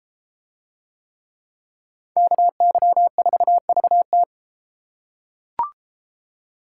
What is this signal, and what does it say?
Morse code, heard as a keyed tone, about 32 wpm, 700 Hz. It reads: KY4VT